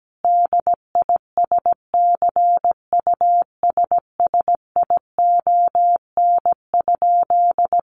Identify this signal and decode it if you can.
Morse code, heard as a keyed tone, about 17 wpm, 700 Hz. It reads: DISCUSSION?
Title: Sentences from Top 1100 Words 17wpm